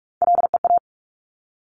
Morse code, heard as a keyed tone, about 46 wpm, 690 Hz. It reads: LEA
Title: Top Three Letter Combinations in Words 45wpm